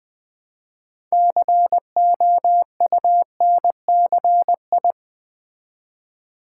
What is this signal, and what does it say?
Morse code, heard as a keyed tone, about 20 wpm, 700 Hz. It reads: COUNCI
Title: Build A Word - 3 Letters At A Time - 9-Letter Words - 20wpm